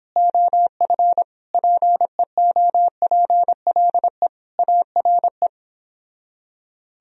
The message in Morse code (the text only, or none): OF PEOPLE ARE